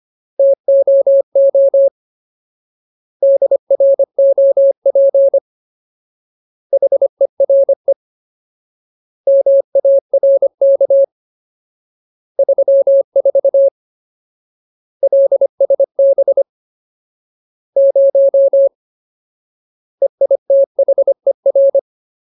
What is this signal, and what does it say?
Morse code, heard as a keyed tone, about 25 wpm, 550 Hz. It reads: TOO DROP HERE MARK 34 LSB 0 EITHER
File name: Ultimate Word List / 25WPM Ultimate